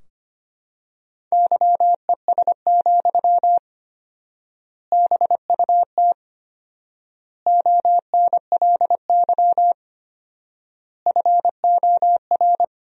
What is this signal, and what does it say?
Morse code, heard as a keyed tone, about 25 wpm, 700 Hz. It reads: YES, BUT ONLY FOR
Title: 4x Word-Spacing - Sentences from Top 500 Words - 25wpm